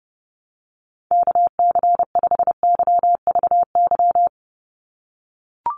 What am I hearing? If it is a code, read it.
Morse code, heard as a keyed tone, about 30 wpm, 700 Hz. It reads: KC5YVY
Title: USA Call Signs 30wpm